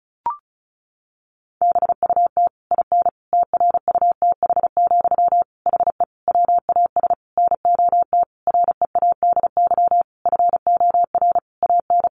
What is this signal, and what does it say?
Morse code, heard as a keyed tone, about 34 wpm, 700 Hz. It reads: BUT IN TRUTH, HE WAS NOT READY FOR AN
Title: Sentences from Top 1500 Words 35wpm